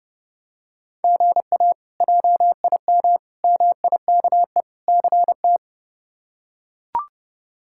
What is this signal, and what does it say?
Morse code, heard as a keyed tone, about 30 wpm, 700 Hz. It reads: GA JIM MIKE CT
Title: K1USN SST QSO Practice - 30wpm